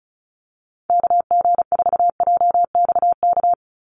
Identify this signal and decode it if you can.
Morse code, heard as a keyed tone, about 34 wpm, 700 Hz. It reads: KG4JXK
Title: USA Call Signs 35wpm